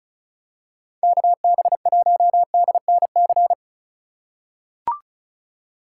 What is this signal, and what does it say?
Morse code, heard as a keyed tone, about 34 wpm, 700 Hz. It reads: KB1DNC
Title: USA Call Signs - 6 Characters - 3x Character Spacing - 35wpm